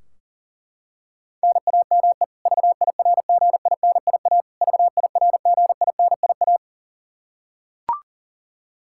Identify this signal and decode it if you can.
Morse code, heard as a keyed tone, about 40 wpm, 700 Hz. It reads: NAME VIRGINIA VIRGINIA
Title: Practice QSO - Exchange 1 - Beginning - 40wpm